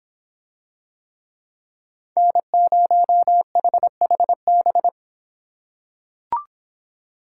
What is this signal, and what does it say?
Morse code, heard as a keyed tone, about 26 wpm, 700 Hz. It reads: N0HHB